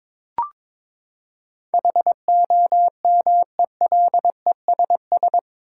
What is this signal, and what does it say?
Morse code, heard as a keyed tone, about 22 wpm, 700 Hz. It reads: HOMELESS
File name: practice / 22wpm / Eight Letter Words - 22wpm